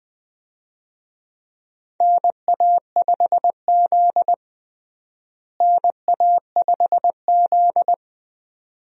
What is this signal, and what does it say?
Morse code, heard as a keyed tone, about 20 wpm, 700 Hz. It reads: NA5Z NA5Z
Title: USA Call Signs - 4 Characters - Repeated 2x - 20wpm